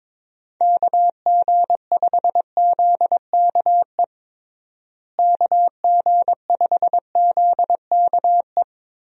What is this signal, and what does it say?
Morse code, heard as a keyed tone, about 22 wpm, 700 Hz. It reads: KG5ZKE KG5ZKE
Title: USA Call Signs - 6 Characters - Repeated 2x - 22wpm